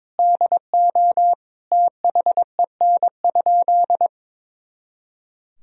Morse code, heard as a keyed tone, about 22 wpm, 700 Hz. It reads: DO THEN?
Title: Sentences from Top 100 Words and Abbreviations 22wpm